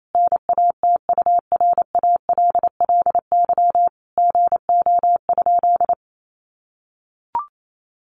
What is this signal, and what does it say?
Morse code, heard as a keyed tone, about 28 wpm, 700 Hz. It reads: NATURALLY GO?
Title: Sentences from Top 1300 Words 28wpm